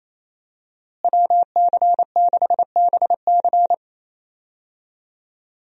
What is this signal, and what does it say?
Morse code, heard as a keyed tone, about 28 wpm, 700 Hz. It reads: WC6BC
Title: USA Call Signs - 5 Characters - Repeated 3x - 28wpm